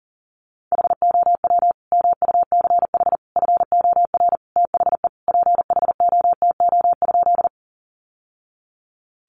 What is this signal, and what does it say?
Morse code, heard as a keyed tone, about 40 wpm, 700 Hz. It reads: HOW MUCH FOR THE PHOTO?
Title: Sentences from Top 1200 Words 40wpm